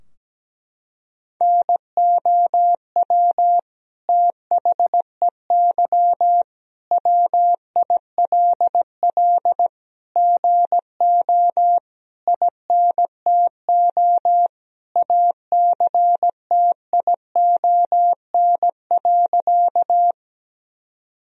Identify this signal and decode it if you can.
Morse code, heard as a keyed tone, about 17 wpm, 700 Hz. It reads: NOW THEY WILL GO INTO ACTION.
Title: Sentences from Top 1000 Words - Review Part 4 - 17wpm